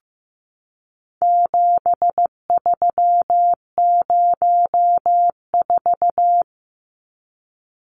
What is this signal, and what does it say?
Morse code, heard as a keyed tone, about 15 wpm, 700 Hz. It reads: 7304
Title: Numbers - 4-Digit 15wpm